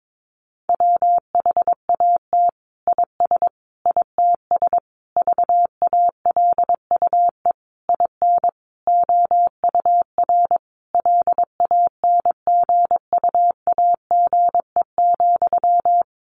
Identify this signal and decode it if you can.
Morse code, heard as a keyed tone, about 22 wpm, 700 Hz. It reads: WHAT IS ITS VALUE IN OUR LANGUAGE,
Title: Sentences from Top 600 Words 22wpm